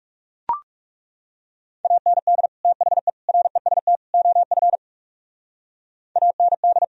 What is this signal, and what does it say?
Morse code, heard as a keyed tone, about 36 wpm, 700 Hz. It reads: AND THE REST OF AND
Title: Speed-Racing - Sets of 4 Words 30wpm